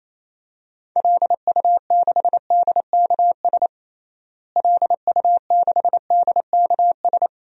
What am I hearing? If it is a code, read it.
Morse code, heard as a keyed tone, about 28 wpm, 700 Hz. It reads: LU6DKS LU6DKS
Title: DXCC-100 Call Signs - 6 Characters - Repeated 3x - 28wpm